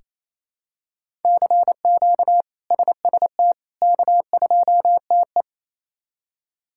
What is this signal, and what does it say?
Morse code, heard as a keyed tone, about 28 wpm, 700 Hz. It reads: CQ SST K2TE
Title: K1USN SST QSO Practice - 28wpm